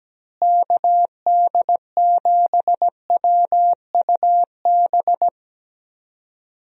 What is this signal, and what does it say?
Morse code, heard as a keyed tone, about 17 wpm, 700 Hz. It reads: KD7WUB
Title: USA Call Signs - 6 Characters - Repeated 3x - 17wpm